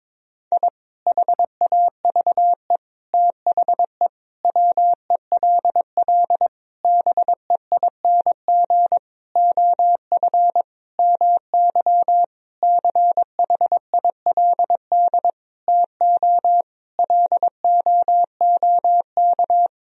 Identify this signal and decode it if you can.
Morse code, heard as a keyed tone, about 22 wpm, 700 Hz. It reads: I HAVE THE WELL BEING OF MY CHILD TO LOOK
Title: Sentences from Top 1000 Words - Review Part 3 - 22wpm